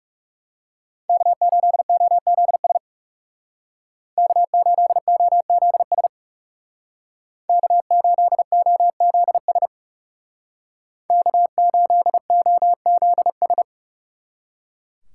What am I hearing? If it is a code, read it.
Morse code, heard as a keyed tone, about 36 wpm, 700 Hz. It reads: K8OZS K8OZS K8OZS K8OZS
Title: Speed Racing - USA Call Signs 30wpm